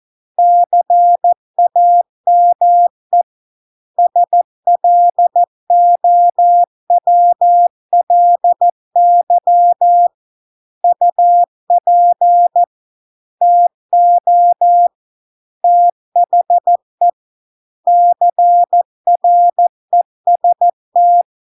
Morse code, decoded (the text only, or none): CAME SLOWLY UP TO THE CREST